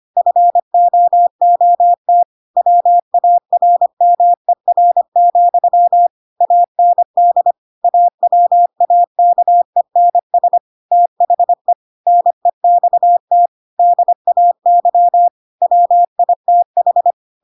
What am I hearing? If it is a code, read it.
Morse code, heard as a keyed tone, about 25 wpm, 700 Hz. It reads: FOOT WARMER, AND AWAKENS THE NEXT DAY WITH
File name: Pinocchio / Pinocchio ch6 25 25 ik1hge